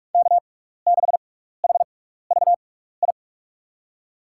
Morse code, heard as a keyed tone, about 42 wpm, spaced as for 15 wpm, 700 Hz. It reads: K6HVI
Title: USA Call Signs - 5 Characters - 6x Character Spacing - 45wpm